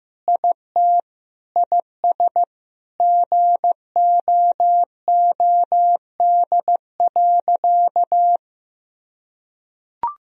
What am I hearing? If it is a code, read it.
Morse code, heard as a keyed tone, about 15 wpm, 700 Hz. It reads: IT IS GOOD.
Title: Sentences from Top 1000 Words - Review Part 3 - 15wpm